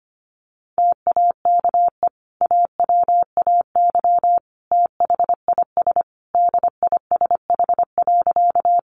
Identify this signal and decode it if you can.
Morse code, heard as a keyed tone, about 25 wpm, 700 Hz. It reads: TAKE AWAY THIS DISH.
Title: Sentences from Top 1300 Words 25wpm